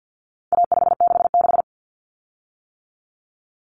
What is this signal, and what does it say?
Morse code, heard as a keyed tone, about 50 wpm, 700 Hz. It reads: A566
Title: Common Cut Numbers - 4-Digit 50wpm